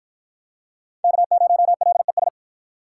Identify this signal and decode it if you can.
Morse code, heard as a keyed tone, about 50 wpm, 700 Hz. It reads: K0LES